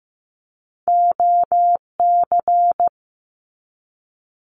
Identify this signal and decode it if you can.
Morse code, heard as a keyed tone, about 15 wpm, 700 Hz. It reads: OC
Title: Abbreviations 15wpm